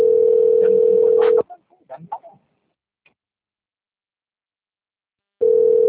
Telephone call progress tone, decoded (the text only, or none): ringback tone